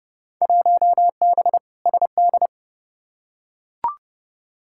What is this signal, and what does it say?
Morse code, heard as a keyed tone, about 30 wpm, 700 Hz. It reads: 1B SD
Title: Field Day Exchanges Encore - 30wpm